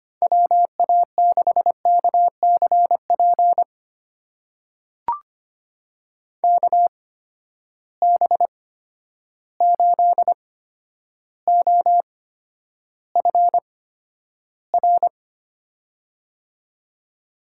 Morse code, heard as a keyed tone, about 25 wpm, spaced as for 11 wpm, 700 Hz. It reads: WA6KCP KB8OFR
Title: USA Call Signs - 6 Characters - 8x Character Spacing - 25wpm